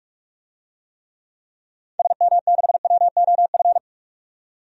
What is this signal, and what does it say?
Morse code, heard as a keyed tone, about 46 wpm, 700 Hz. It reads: SM6WOF
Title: DXCC-100 Call Signs - 6 Characters - 45wpm